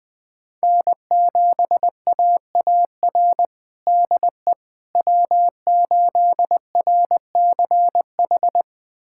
Morse code, heard as a keyed tone, about 21 wpm, 700 Hz. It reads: N7AAR DE W8RCH
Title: USA Callsign DE Callsign - 5 Characters - 20wpm